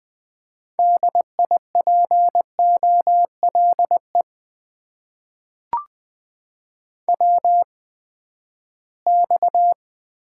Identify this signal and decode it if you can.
Morse code, heard as a keyed tone, about 20 wpm, spaced as for 14 wpm, 700 Hz. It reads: DIPOLE WX
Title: Top 90 QSO Elements - 8x Character Spacing - 20wpm